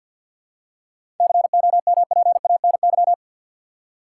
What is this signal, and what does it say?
Morse code, heard as a keyed tone, about 51 wpm, 700 Hz. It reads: COMPANY
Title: Seven Letter Words - 50wpm